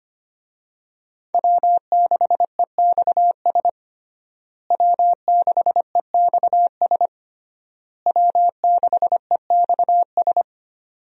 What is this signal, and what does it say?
Morse code, heard as a keyed tone, about 25 wpm, 700 Hz. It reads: W6EXS W6EXS W6EXS
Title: USA Call Signs - 5 Characters - Repeated 3x - 25wpm